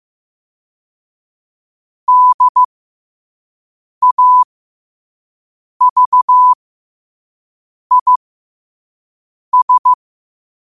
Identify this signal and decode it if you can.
Morse code, heard as a keyed tone, about 15 wpm, spaced as for 5 wpm, 1000 Hz. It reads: DAVIS
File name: qotdfiles / 5wpm